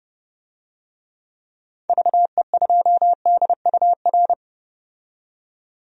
Morse code, heard as a keyed tone, about 30 wpm, 700 Hz. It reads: VE2DUR
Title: DXCC-100 Call Signs - 6 Characters - 30wpm